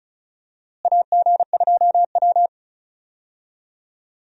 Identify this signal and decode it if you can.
Morse code, heard as a keyed tone, about 36 wpm, 700 Hz. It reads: AG2W
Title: USA Call Signs - 4 Characters - Repeated 3x - 35wpm